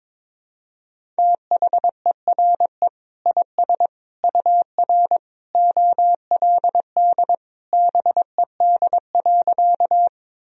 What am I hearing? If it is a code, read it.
Morse code, heard as a keyed tone, about 22 wpm, 700 Hz. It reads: THERE IS UR OLD BED.
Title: Sentences from Top 500 Words and Abbreviations 22wpm